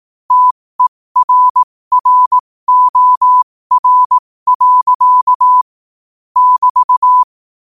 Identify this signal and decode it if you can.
Morse code, heard as a keyed tone, about 18 wpm, spaced as for 15 wpm, 1000 Hz. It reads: TERROR. <BT>